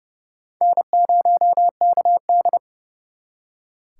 Morse code, heard as a keyed tone, about 39 wpm, 700 Hz. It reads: N0KD